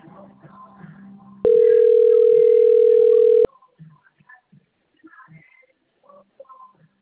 Ringback tone (North America), 440 + 480 Hz.
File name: out- - - - - .54538